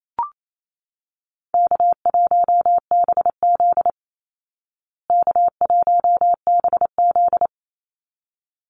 Morse code, heard as a keyed tone, about 28 wpm, 700 Hz. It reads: K1BZ K1BZ
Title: Call Signs Repeated 3x - 28wpm